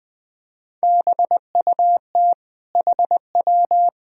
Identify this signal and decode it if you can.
Morse code, heard as a keyed tone, about 20 wpm, 700 Hz. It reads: BUT HW
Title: Sentences from Top 400 Words and Abbreviations 20wpm